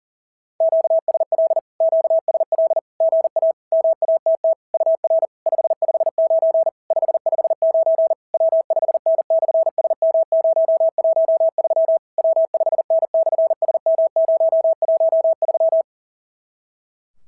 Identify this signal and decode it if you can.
Morse code, heard as a keyed tone, about 42 wpm, 650 Hz. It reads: QSL QSL GA MATT UR 559 559 W5N/SM013 W5N/SM013
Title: SOTA QSO Practice 40wpm